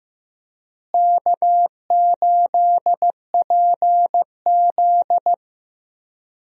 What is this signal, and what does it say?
Morse code, heard as a keyed tone, about 15 wpm, 700 Hz. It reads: K8PZ
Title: USA Call Signs - 4 Characters - Repeated 3x - 15wpm